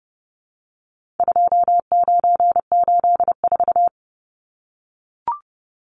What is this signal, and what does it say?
Morse code, heard as a keyed tone, about 30 wpm, 700 Hz. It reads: 2984
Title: Numbers - 4-Digit 30wpm